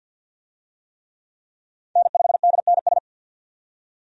Morse code, heard as a keyed tone, about 50 wpm, 700 Hz. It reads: N5DNS
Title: USA Call Signs - 5 Characters - 50wpm